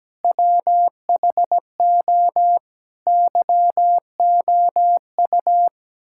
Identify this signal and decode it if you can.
Morse code, heard as a keyed tone, about 17 wpm, 700 Hz. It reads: WHO YOU
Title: Sentences from Top 1000 Words - Review Part 6 - 17wpm